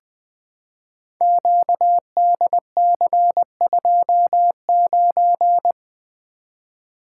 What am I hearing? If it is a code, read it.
Morse code, heard as a keyed tone, about 20 wpm, 700 Hz. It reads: QDC29
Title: Code Groups - 5 Letter:Numbers - 20wpm